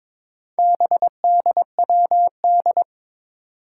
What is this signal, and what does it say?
Morse code, heard as a keyed tone, about 22 wpm, 700 Hz. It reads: BDWD
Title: Four Challenging Letter-Number Combinations 22wpm